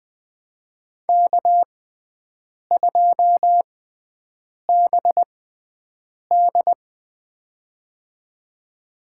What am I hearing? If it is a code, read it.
Morse code, heard as a keyed tone, about 20 wpm, spaced as for 7 wpm, 700 Hz. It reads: K2BD
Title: USA Call Signs - 6x Character Spacing - 20wpm